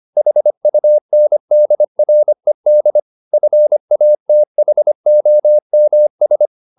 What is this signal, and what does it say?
Morse code, heard as a keyed tone, about 25 wpm, 600 Hz. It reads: HUNDRED FATHOMS
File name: the old man and the sea part 1 25wpm